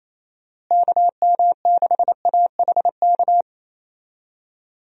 Morse code, heard as a keyed tone, about 28 wpm, 700 Hz. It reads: KM6AHK
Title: USA Call Signs - 6 Characters - Repeated 3x - 28wpm